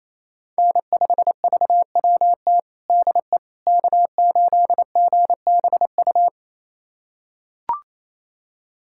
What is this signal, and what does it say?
Morse code, heard as a keyed tone, about 28 wpm, 700 Hz. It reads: N5VWT DE K8GBU
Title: USA Callsign DE Callsign - 5 Characters - 28wpm